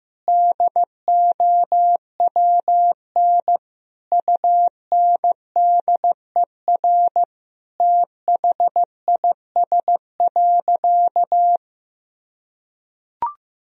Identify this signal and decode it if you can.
Morse code, heard as a keyed tone, about 15 wpm, 700 Hz. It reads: DOWN UNDER THIS.
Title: Sentences from Top 1000 Words - Review Part 1 - 15wpm